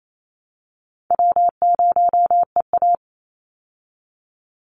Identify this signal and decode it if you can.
Morse code, heard as a keyed tone, about 28 wpm, 700 Hz. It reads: W0EA
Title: Easy Call Signs Repeated 3x - 28wpm